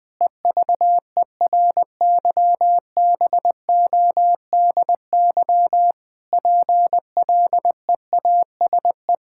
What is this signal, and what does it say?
Morse code, heard as a keyed tone, about 20 wpm, 700 Hz. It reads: EVERYBODY PLEASE
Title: Sentences from Top 1600 Words 20wpm